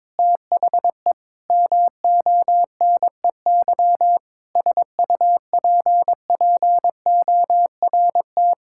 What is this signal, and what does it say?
Morse code, heard as a keyed tone, about 22 wpm, 700 Hz. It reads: THE MONEY SUPPORT
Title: Sentences from Top 1000 Words - Review Part 5 - 22wpm